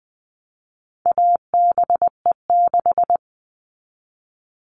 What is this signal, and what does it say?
Morse code, heard as a keyed tone, about 20 wpm, 700 Hz. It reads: ABE6
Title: All Cut Numbers - 4-Digit 20wpm